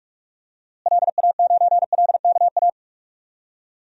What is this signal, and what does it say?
Morse code, heard as a keyed tone, about 43 wpm, 700 Hz. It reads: LA9LKA
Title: DXCC-100 Call Signs - 6 Characters - Repeated 2x - 45wpm